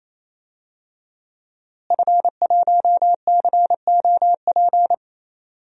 Morse code, heard as a keyed tone, about 28 wpm, 690 Hz. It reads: F1COP
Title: DXCC-100 Call Signs - 5 Characters - Repeated 3x - 28wpm